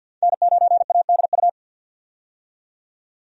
Morse code, heard as a keyed tone, about 48 wpm, 700 Hz. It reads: N9ADU
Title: USA Call Signs - 5 Characters - 50wpm